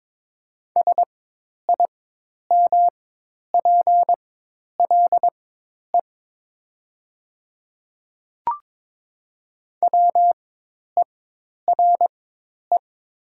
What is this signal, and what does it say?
Morse code, heard as a keyed tone, about 22 wpm, spaced as for 10 wpm, 700 Hz. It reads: SIMPLE WERE
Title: 4x Character Spacing - Top 500 Words - 22wpm